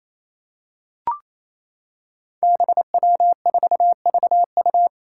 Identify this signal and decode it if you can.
Morse code, heard as a keyed tone, about 28 wpm, 700 Hz. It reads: BW4VU
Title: Five Challenging Letter-Number Combinations 28wpm